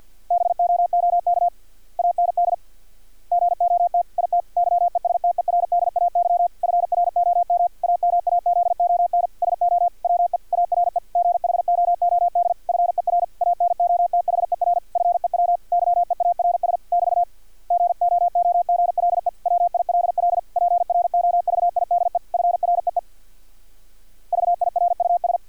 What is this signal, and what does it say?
Morse code, heard as a keyed tone, about 50 wpm, 700 Hz. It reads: BOOK AND GOT IT YESTERDAY FROM AMAZON SO WE ARE GHOOD FER ANOTHER FEW YEARS <BT> GOOGLE WILL PROVIDE FREE VIRUS
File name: archived / CRSnet